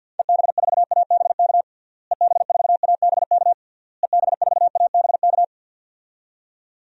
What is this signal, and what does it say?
Morse code, heard as a keyed tone, about 53 wpm, 700 Hz. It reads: EB4ABK EB4ABK EB4ABK
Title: DXCC-100 Call Signs - 6 Characters - Repeated 3x - 50wpm